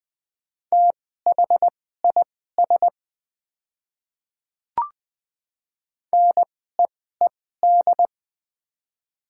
Morse code, heard as a keyed tone, about 20 wpm, spaced as for 15 wpm, 700 Hz. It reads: THIS NEED
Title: 2x Character Spacing - Top 200 Words - 20wpm